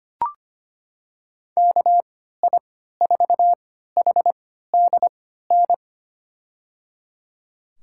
Morse code, heard as a keyed tone, about 25 wpm, spaced as for 14 wpm, 700 Hz. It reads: KI4HDN
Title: USA Call Signs - 6 Characters - 3x Character Spacing - 25wpm